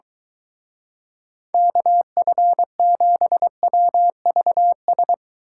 Morse code, heard as a keyed tone, about 23 wpm, 700 Hz. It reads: KF7WVS